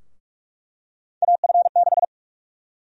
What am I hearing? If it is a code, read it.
Morse code, heard as a keyed tone, about 45 wpm, 700 Hz. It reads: AF6
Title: USA Call Sign Prefixes - 45wpm